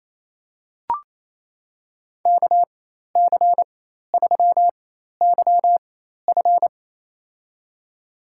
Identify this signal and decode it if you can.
Morse code, heard as a keyed tone, about 27 wpm, spaced as for 13 wpm, 700 Hz. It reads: KC3YF
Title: USA Call Signs - 5 Characters - 4x Character Spacing - 28wpm